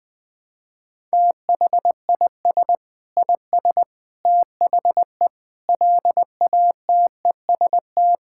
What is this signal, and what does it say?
Morse code, heard as a keyed tone, about 21 wpm, 700 Hz. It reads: THIS IS THE LATEST
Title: Sentences with 6-Word Phrases - 20wpm